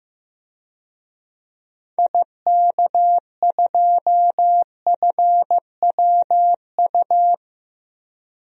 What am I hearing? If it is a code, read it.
Morse code, heard as a keyed tone, about 15 wpm, 700 Hz. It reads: IK2FWU
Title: DXCC-100 Call Signs - 6 Characters - 15wpm